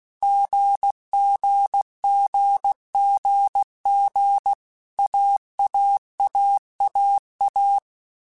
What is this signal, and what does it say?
Morse code, heard as a keyed tone, about 16 wpm, 800 Hz. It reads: GGGGG AAAAA
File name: Basic Letter Groups By7Groups / AWUNDG / AWUNDG@15